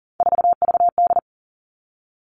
Morse code, heard as a keyed tone, about 40 wpm, 700 Hz. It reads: 4VD
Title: All Cut Numbers - 3-Digit 40wpm